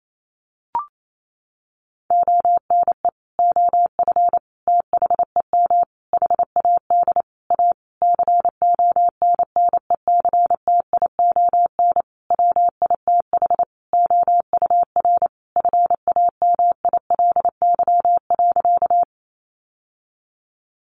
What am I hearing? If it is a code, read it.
Morse code, heard as a keyed tone, about 28 wpm, 700 Hz. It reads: ONE OF THEM HAD A CONNECTION WITH OUR FAMILY.
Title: Sentences from Top 1300 Words 28wpm